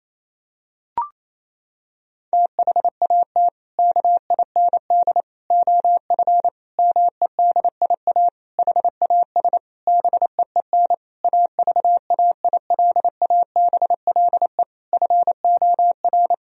Morse code, heard as a keyed tone, about 28 wpm, 700 Hz. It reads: THAT KIND OF MEDIA HAS BEEN AVAILABLE FOR
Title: Sentences from Top 1000 Words - Review Part 4 - 28wpm